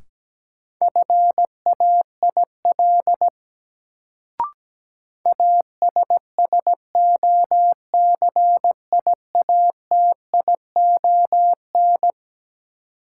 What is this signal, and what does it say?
Morse code, heard as a keyed tone, about 17 wpm, 700 Hz. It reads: FAIL ASSOCIATION
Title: Sentences from Top 1100 Words - New Word Drill - 17wpm